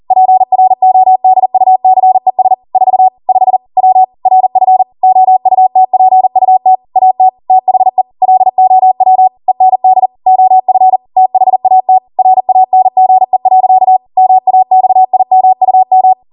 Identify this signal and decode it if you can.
Morse code, heard as a keyed tone, about 40 wpm, 750 Hz. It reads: PRODUCES 4 5 W RF OUTPUT AT THE LOW END OF THAT RANGE. MAXIMUM
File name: Morse / 40WPM